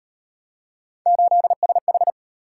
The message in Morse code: 8SH

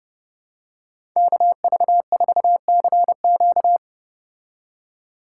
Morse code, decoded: KV4CQ

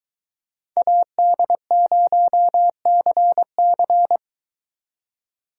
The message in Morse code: AD0CC

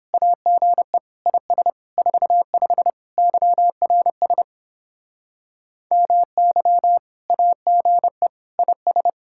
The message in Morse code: AGE IS 45 YRS MY AGE IS

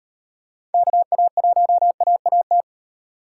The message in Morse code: KA1AAT

